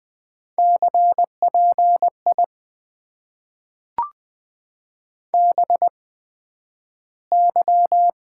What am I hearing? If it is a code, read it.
Morse code, heard as a keyed tone, about 20 wpm, spaced as for 11 wpm, 700 Hz. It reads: CPI BY
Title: Top 90 QSO Elements - 8x Character Spacing - 20wpm